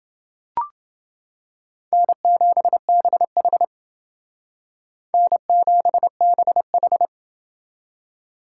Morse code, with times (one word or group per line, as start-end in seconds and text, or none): 1.9-3.6 N7BH
5.1-7.1 N7BH